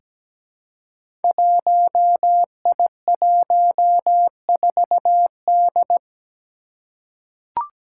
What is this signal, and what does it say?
Morse code, heard as a keyed tone, about 17 wpm, 700 Hz. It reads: 1I14D
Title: Five Challenging Letter-Number Combinations 17wpm